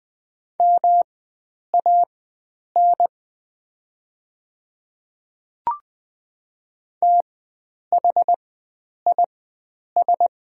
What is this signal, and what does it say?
Morse code, heard as a keyed tone, about 20 wpm, spaced as for 9 wpm, 700 Hz. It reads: MAN THIS
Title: 4x Character Spacing - Top 300 Words - 20wpm